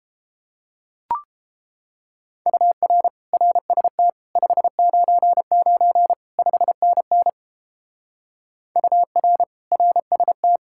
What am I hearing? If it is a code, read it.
Morse code, heard as a keyed tone, about 29 wpm, 700 Hz. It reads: UR RST 599 5NN UR RST